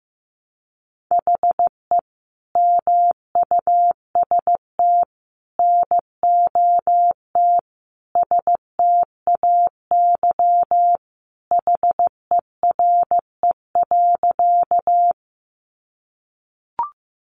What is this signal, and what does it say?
Morse code, heard as a keyed tone, about 14 wpm, 700 Hz. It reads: HE MUST NOT STAY HERE.
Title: Sentences from Top 600 Words 15wpm